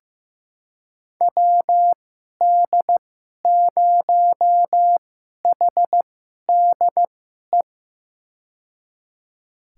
Morse code, heard as a keyed tone, about 15 wpm, spaced as for 11 wpm, 700 Hz. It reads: WD0HDE
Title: USA Call Signs - 6 Characters - 2x Character Spacing - 15wpm